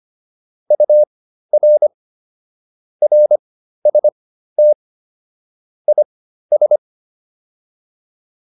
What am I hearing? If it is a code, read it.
Morse code, heard as a keyed tone, about 25 wpm, spaced as for 13 wpm, 600 Hz. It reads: UR RST IS